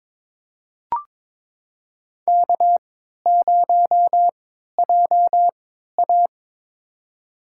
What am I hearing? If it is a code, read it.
Morse code, heard as a keyed tone, about 22 wpm, spaced as for 13 wpm, 700 Hz. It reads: K0JA
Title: USA Call Signs - 3x Character Spacing - 22wpm